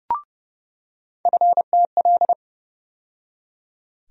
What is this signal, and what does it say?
Morse code, heard as a keyed tone, about 30 wpm, 700 Hz. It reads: FTL
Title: USA Call Sign Postfixes - 3 Letters 30wpm